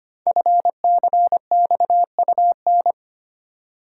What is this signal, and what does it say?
Morse code, heard as a keyed tone, about 25 wpm, 700 Hz. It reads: FCXUN
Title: Code Groups - 5 Letters - 25wpm